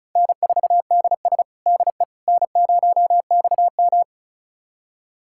Morse code, heard as a keyed tone, about 34 wpm, 700 Hz. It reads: N4DS DE N0XM
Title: USA Callsign DE Callsign - 4 Characters - 35wpm